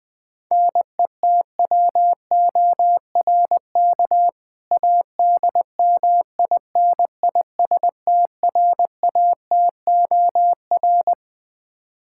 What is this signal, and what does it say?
Morse code, heard as a keyed tone, about 20 wpm, 700 Hz. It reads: NETWORK ADMINISTRATOR
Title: K1USN SST QSO Practice 20wpm